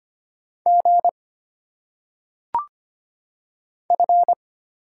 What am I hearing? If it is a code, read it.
Morse code, heard as a keyed tone, about 25 wpm, 700 Hz. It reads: G F
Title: Single Letters - Phonetic Pronunciation 25wpm